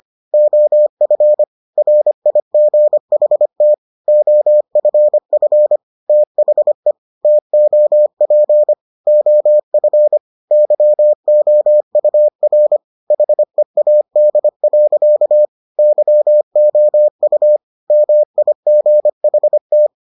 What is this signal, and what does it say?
Morse code, heard as a keyed tone, about 25 wpm, 600 Hz. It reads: OF RIGHT OFF THE TOP OF YOUR HEAD. YOU MIGHT